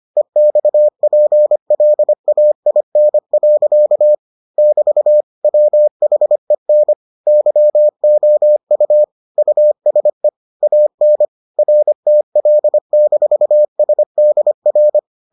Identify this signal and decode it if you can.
Morse code, heard as a keyed tone, about 25 wpm, 600 Hz. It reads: EXPLAIN. <BT> WHEN YOU USE AN RTL-SDR